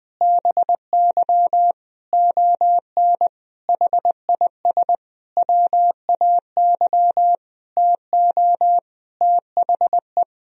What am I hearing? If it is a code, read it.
Morse code, heard as a keyed tone, about 21 wpm, 700 Hz. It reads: BY ON HIS WAY TO THE
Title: Sentences from Top 1000 Words - Review Part 1 - 20wpm